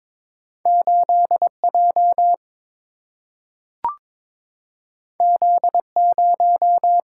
Morse code, heard as a keyed tone, about 22 wpm, 700 Hz. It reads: 8J Z0
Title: Single-Letter Single-Number 22wpm